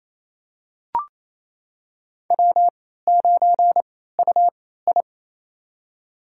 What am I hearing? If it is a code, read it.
Morse code, heard as a keyed tone, about 28 wpm, spaced as for 16 wpm, 700 Hz. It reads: W9UI
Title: USA Call Signs - 3x Character Spacing - 28wpm